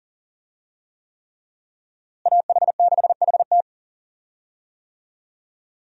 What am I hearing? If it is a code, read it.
Morse code, heard as a keyed tone, about 37 wpm, 700 Hz. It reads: AH6HT
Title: USA Call Signs - 5 Characters - Repeated 3x - 40wpm